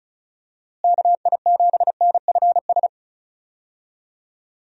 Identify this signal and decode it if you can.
Morse code, heard as a keyed tone, about 34 wpm, 700 Hz. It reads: KI7NFS
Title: USA Call Signs - 6 Characters - Repeated 2x - 35wpm